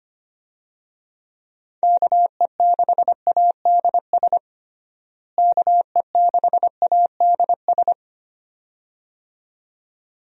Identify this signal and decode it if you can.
Morse code, heard as a keyed tone, about 25 wpm, 700 Hz. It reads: KE6ADS KE6ADS
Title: USA Call Signs - 6 Characters - Repeated 2x - 25wpm